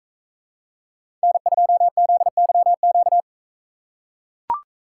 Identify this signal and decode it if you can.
Morse code, heard as a keyed tone, about 44 wpm, 700 Hz. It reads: N2ZYQ